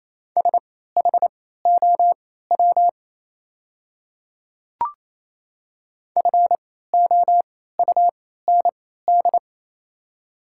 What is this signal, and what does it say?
Morse code, heard as a keyed tone, about 28 wpm, spaced as for 16 wpm, 700 Hz. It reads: SHOW FOUND
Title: 3x Character Spacing - Top 300 Words - 28wpm